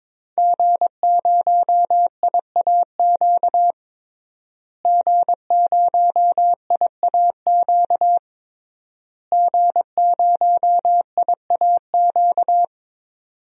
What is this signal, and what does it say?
Morse code, heard as a keyed tone, about 22 wpm, 700 Hz. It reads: G0IAQ G0IAQ G0IAQ
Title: DXCC-100 Call Signs - 5 Characters - Repeated 3x - 22wpm